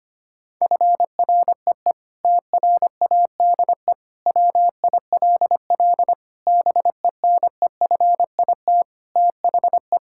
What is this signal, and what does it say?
Morse code, heard as a keyed tone, about 25 wpm, 700 Hz. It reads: FREE TRADE WILL BENEFIT THE
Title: Sentences from Top 1000 Words - Review Part 6 - 25wpm